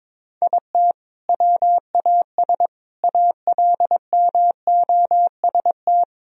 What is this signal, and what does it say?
Morse code, heard as a keyed tone, about 22 wpm, 700 Hz. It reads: IT WAS ALMOST
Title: Sentences from Top 1000 Words - Review Part 4 - 22wpm